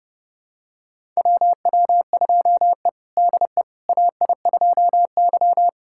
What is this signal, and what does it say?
Morse code, heard as a keyed tone, about 32 wpm, 700 Hz. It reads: WW2E DE AI2Y
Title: USA Callsign DE Callsign - 4 Characters - 30wpm